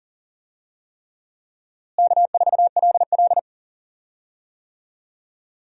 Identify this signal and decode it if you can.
Morse code, heard as a keyed tone, about 40 wpm, 700 Hz. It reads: K4LL
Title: USA Call Signs - 4 Characters - Repeated 3x - 40wpm